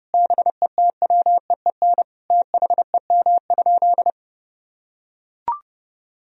Morse code, heard as a keyed tone, about 32 wpm, 700 Hz. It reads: BETWEEN THEM?
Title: Sentences from Top 1000 Words - Review Part 4 - 30wpm